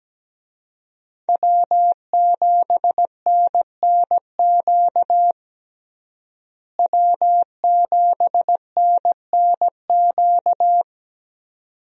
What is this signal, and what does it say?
Morse code, heard as a keyed tone, about 17 wpm, 700 Hz. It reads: W7NNQ W7NNQ
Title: USA Call Signs - 5 Characters - Repeated 2x - 17wpm